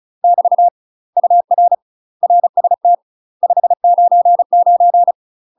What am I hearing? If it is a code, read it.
Morse code, heard as a keyed tone, about 35 wpm, 700 Hz. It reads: <BT> UR RST 599
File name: UK MASTER QSOs TRAINING / 35WPM / UK MASTER QSOs 0018